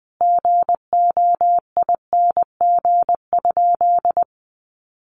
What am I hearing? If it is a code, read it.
Morse code, heard as a keyed tone, about 20 wpm, 700 Hz. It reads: GOING?
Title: Sentences from Top 1100 Words 20wpm